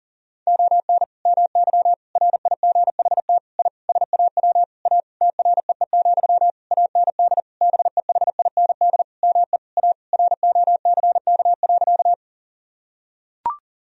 Morse code, heard as a keyed tone, about 40 wpm, 700 Hz. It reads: ON MY RIGHT I SAW A TREE, AND BEHIND ME A ROCK.
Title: Sentences from Top 1000 Words - Review Part 2 - 40wpm